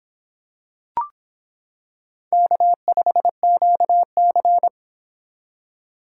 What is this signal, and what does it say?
Morse code, heard as a keyed tone, about 25 wpm, 700 Hz. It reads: K5QC